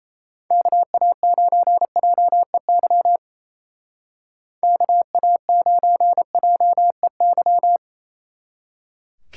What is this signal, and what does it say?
Morse code, heard as a keyed tone, about 34 wpm, 700 Hz. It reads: KA9JEY KA9JEY
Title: Speed Racing - USA Call Signs 28wpm